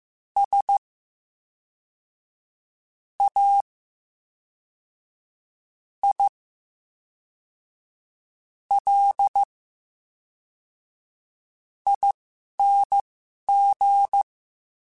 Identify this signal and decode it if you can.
Morse code, heard as a keyed tone, about 15 wpm, spaced as for 4 wpm, 800 Hz. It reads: SAILING
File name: Special INGs / Special INGs-mp3s / VerySlowToFaster Ing @10